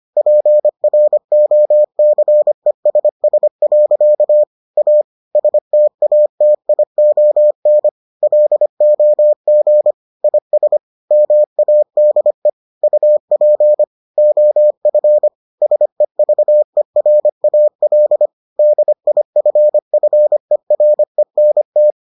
Morse code, heard as a keyed tone, about 25 wpm, 600 Hz. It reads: PROCESS. A STATION LOG IS MADE UP OF SEVERAL DIFFERENT